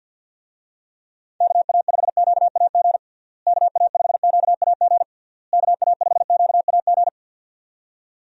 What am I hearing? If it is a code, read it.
Morse code, heard as a keyed tone, about 52 wpm, 700 Hz. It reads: KA5QAG KA5QAG KA5QAG
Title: USA Call Signs - 6 Characters - Repeated 3x - 50wpm